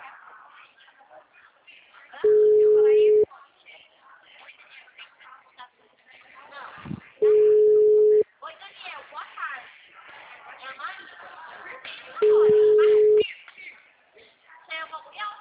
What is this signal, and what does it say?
Ringback tone (Europe, 425 Hz), with speech.
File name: out- -2005- - - .9510